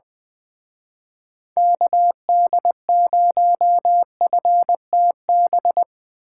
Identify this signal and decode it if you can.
Morse code, heard as a keyed tone, about 23 wpm, 700 Hz. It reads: KD0FTB
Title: Speed Racing - USA 2x3 Call Signs 20wpm